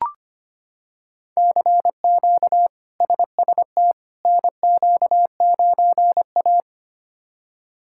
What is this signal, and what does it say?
Morse code, heard as a keyed tone, about 25 wpm, 700 Hz. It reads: CQ SST NQ9A
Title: K1USN SST QSO Practice - 25wpm